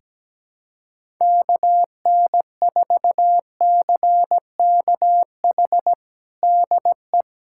Morse code, heard as a keyed tone, about 17 wpm, 700 Hz. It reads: KN4CKH DE
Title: USA Callsign DE Callsign 17wpm